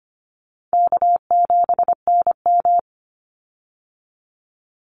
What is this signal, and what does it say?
Morse code, heard as a keyed tone, about 25 wpm, 700 Hz. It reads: K7NM
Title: Call Signs Repeated 3x - 25wpm